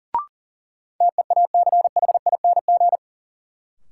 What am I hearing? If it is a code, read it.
Morse code, heard as a keyed tone, about 39 wpm, 700 Hz. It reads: TEACHING
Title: Sentences from Top 1100 Words - New Word Drill - 40wpm